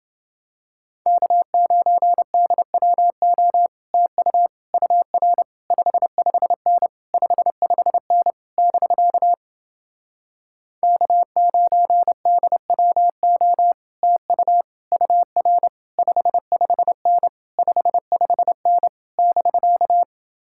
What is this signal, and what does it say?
Morse code, heard as a keyed tone, about 30 wpm, 700 Hz. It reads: K9DWO TU UR 55N 55N <BK> K9DWO TU UR 55N 55N <BK>